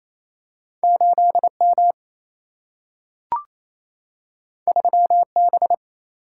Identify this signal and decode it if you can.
Morse code, heard as a keyed tone, about 28 wpm, 700 Hz. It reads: 8M 3B
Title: Single-Letter Single-Number 28wpm